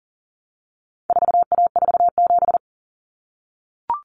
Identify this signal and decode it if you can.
Morse code, heard as a keyed tone, about 40 wpm, 700 Hz. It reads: 4A47
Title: Common Cut Numbers - 4-Digit 40wpm